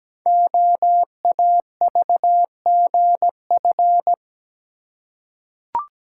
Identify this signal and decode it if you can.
Morse code, heard as a keyed tone, about 17 wpm, 700 Hz. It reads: OAVGF
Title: Code Groups - 5 Letters - 17wpm